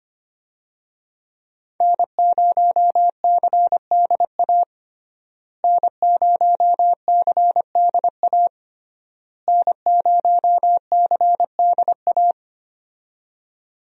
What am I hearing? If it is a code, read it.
Morse code, heard as a keyed tone, about 25 wpm, 700 Hz. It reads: N0CDA N0CDA N0CDA
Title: USA Call Signs - 5 Characters - Repeated 3x - 25wpm